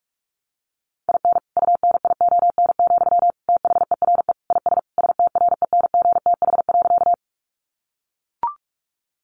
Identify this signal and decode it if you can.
Morse code, heard as a keyed tone, about 44 wpm, 700 Hz. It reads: IN UNION, THERE IS STRENGTH.
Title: Sentences from Top 1400 Words 45wpm